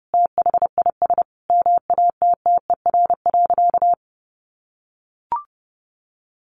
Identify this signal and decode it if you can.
Morse code, heard as a keyed tone, about 30 wpm, 700 Hz. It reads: THIS MATTER.
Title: Sentences from Top 1500 Words 30wpm